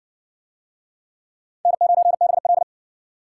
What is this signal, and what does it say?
Morse code, heard as a keyed tone, about 58 wpm, 700 Hz. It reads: N9BL